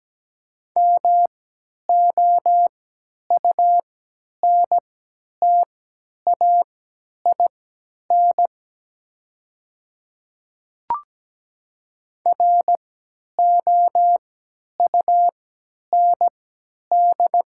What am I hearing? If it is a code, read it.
Morse code, heard as a keyed tone, about 17 wpm, spaced as for 10 wpm, 700 Hz. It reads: MOUNTAIN ROUND
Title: 3x Character Spacing - Top 300 Words - 17wpm